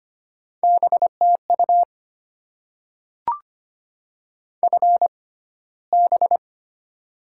Morse code, heard as a keyed tone, about 25 wpm, spaced as for 15 wpm, 700 Hz. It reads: BTU FB
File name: Top 20 QSO Elements - 6x Character Spacing - 25wpm